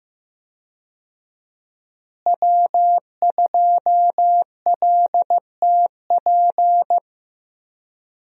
Morse code, heard as a keyed tone, about 15 wpm, 700 Hz. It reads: W2LTP
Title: USA Call Signs - 5 Characters - Repeated 3x - 15wpm